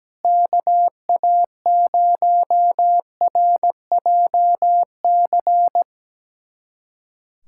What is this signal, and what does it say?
Morse code, heard as a keyed tone, about 17 wpm, 700 Hz. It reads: KA0RJC
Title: USA Callsign DE Callsign - 6 Characters - 17wpm